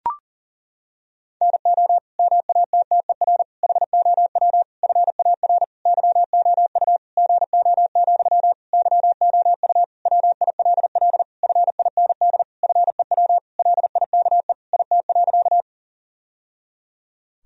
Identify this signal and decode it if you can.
Morse code, heard as a keyed tone, about 37 wpm, 700 Hz. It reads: NO MATTER HOW FAR YOU GO, YOU WILL FIND FEW LIKE IT.
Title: Sentences from Top 1000 Words - Review Part 1 - 40wpm